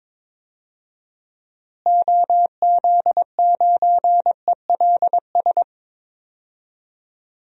Morse code, heard as a keyed tone, about 22 wpm, 700 Hz. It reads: OZ9ELS
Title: DXCC-100 Call Signs - 6 Characters - Repeated 2x - 22wpm